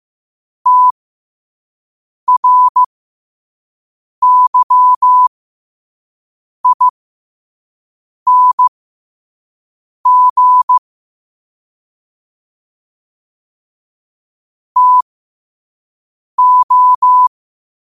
Morse code, heard as a keyed tone, about 15 wpm, spaced as for 5 wpm, 1000 Hz. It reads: TRYING TO
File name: qotdfiles / 5wpm